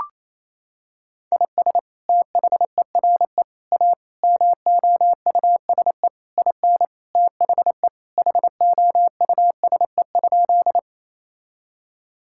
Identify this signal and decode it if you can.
Morse code, heard as a keyed tone, about 27 wpm, 700 Hz. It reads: IS THERE A MOUSE IN THE HOUSE?
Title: Sentences from Top 1000 Words - Review Part 5 - 28wpm